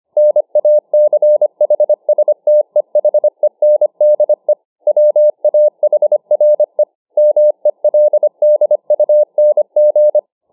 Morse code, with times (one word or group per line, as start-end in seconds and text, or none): 0.2-4.5 NACHSTEHENDE
4.9-6.8 WAHRE
7.2-10.2 MELDUNG